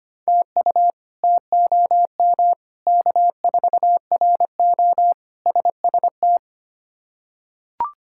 Morse code, heard as a keyed tone, about 25 wpm, 700 Hz. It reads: TU TOM K4RO SST
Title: K1USN SST QSO Practice - 25wpm